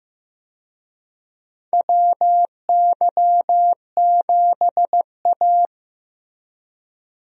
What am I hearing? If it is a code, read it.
Morse code, heard as a keyed tone, about 15 wpm, 700 Hz. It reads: WY7A
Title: USA Call Signs - 4 Characters - Repeated 3x - 15wpm